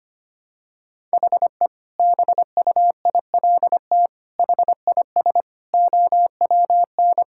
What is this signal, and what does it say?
Morse code, heard as a keyed tone, about 25 wpm, 700 Hz. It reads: HE BUILT HIS OWN
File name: practice / 25wpm / Sentences with 2-Word Phrases - 25wpm